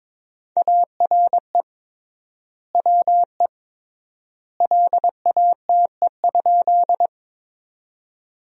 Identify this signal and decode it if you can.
Morse code, heard as a keyed tone, about 22 wpm, 700 Hz. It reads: ARE WE LATE?
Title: 3x Word-Spacing - Sentences from Top 300 Words - 22wpm